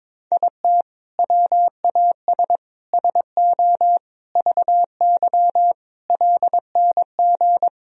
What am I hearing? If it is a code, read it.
Morse code, heard as a keyed tone, about 22 wpm, 700 Hz. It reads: IT WAS SO VY LNG
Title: Sentences from Top 500 Words and Abbreviations 22wpm